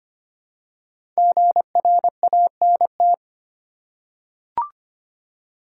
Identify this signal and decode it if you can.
Morse code, heard as a keyed tone, about 25 wpm, 700 Hz. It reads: GRANT
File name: 5 Letter Words - First 3 Letters Given - 25wpm